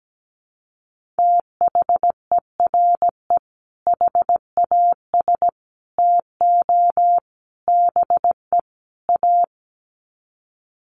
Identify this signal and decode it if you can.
Morse code, heard as a keyed tone, about 17 wpm, 700 Hz. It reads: THERE HAS TO BE A
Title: Sets of 5 Words Encore 17wpm